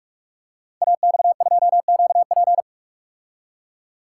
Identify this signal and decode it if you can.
Morse code, heard as a keyed tone, about 44 wpm, 700 Hz. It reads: AX2QP
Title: Code Groups - 5 Letter:Numbers - 45wpm